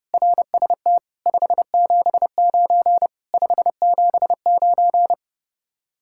Morse code, heard as a keyed tone, about 35 wpm, 700 Hz. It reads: RST 579 579
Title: Speed-Racing Common QSO Phrases - 30wpm